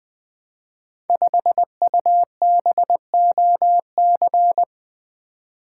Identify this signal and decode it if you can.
Morse code, heard as a keyed tone, about 20 wpm, 700 Hz. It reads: 5UBOC
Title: Code Groups - 5 Letter:Numbers - 20wpm